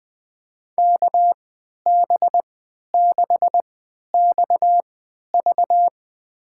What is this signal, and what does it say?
Morse code, heard as a keyed tone, about 20 wpm, spaced as for 11 wpm, 700 Hz. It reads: KB6XV
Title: USA Call Signs - 5 Characters - 3x Character Spacing - 20wpm